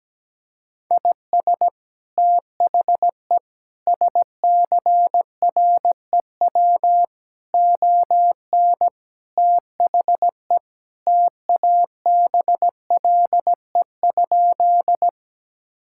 Morse code, heard as a keyed tone, about 17 wpm, 700 Hz. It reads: IS THE SCREW ON THE TABLE?
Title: Sentences from Top 1600 Words 17wpm